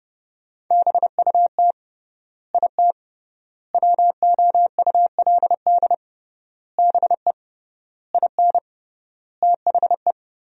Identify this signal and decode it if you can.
Morse code, heard as a keyed tone, about 32 wpm, 700 Hz. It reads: BUT IT WOULD BE IN THE
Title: 3x Word-Spacing - Sentences from Top 300 Words - 30wpm